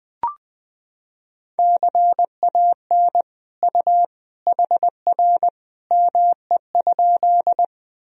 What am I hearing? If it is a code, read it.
Morse code, heard as a keyed tone, about 20 wpm, 700 Hz. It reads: CAN U HR ME?
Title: Sentences from Top 300 Words and Abbreviations 20wpm